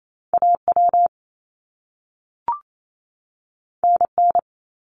Morse code, heard as a keyed tone, about 28 wpm, 700 Hz. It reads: AW NN
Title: Top Two Letter Combinations in Words 28wpm